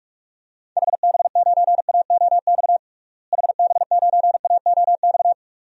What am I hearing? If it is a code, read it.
Morse code, heard as a keyed tone, about 46 wpm, 700 Hz. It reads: HB9AOX HB9AOX
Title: DXCC-100 Call Signs - 6 Characters - Repeated 2x - 45wpm